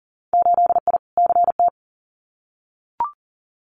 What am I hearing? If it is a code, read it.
Morse code, heard as a keyed tone, about 40 wpm, 700 Hz. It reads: 8I CT
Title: Winter Field Day Exchanges Encore 40wpm